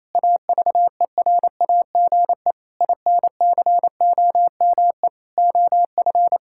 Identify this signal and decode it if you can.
Morse code, heard as a keyed tone, about 28 wpm, 700 Hz. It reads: AVERAGE INCOME OF
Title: Sentences from Top 1000 Words - Review Part 6 - 28wpm